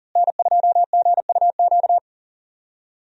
Morse code, heard as a keyed tone, about 38 wpm, 700 Hz. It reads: N2GUQ
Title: USA Call Signs - 5 Characters - Repeated 2x - 40wpm